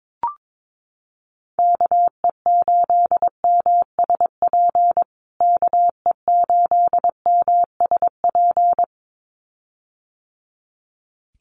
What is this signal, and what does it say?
Morse code, heard as a keyed tone, about 22 wpm, 700 Hz. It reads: KE8MSP KE8MSP
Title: USA Call Signs Repeated 2x 22wpm